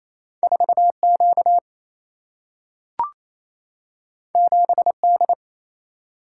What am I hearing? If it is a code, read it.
Morse code, heard as a keyed tone, about 28 wpm, 700 Hz. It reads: 4Q 7D
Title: Single-Letter Single-Number 28wpm